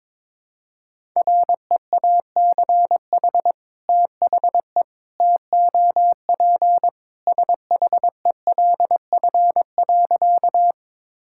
Morse code, heard as a keyed tone, about 22 wpm, 700 Hz. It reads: REACH THE TOP SHELF.
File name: practice / 22wpm / Sentences with 2-Word Phrases - 22wpm